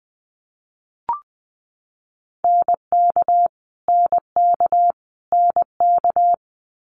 Morse code, heard as a keyed tone, about 20 wpm, 700 Hz. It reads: NK NK NK
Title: Multi-letter Phonemes Repeated 3x 20wpm